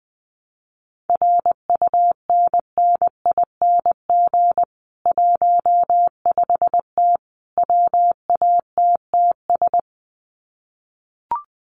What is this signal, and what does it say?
Morse code, heard as a keyed tone, about 20 wpm, 700 Hz. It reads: RUNNING 15T WATTS
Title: Common QSO Phrases 20wpm